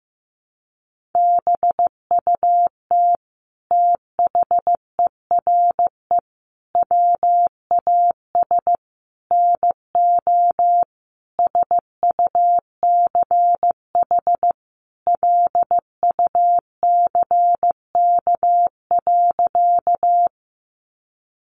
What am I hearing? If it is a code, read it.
Morse code, heard as a keyed tone, about 15 wpm, 700 Hz. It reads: BUT THERE WAS NO SUCH LUCK.
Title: Sentences from Top 1500 Words 15wpm